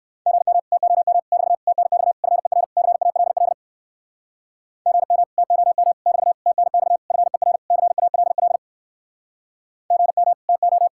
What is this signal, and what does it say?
Morse code, heard as a keyed tone, about 65 wpm, 700 Hz. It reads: GM TOM <BT> TNX FER CALL GM TOM <BT> TNX FER CALL GM TO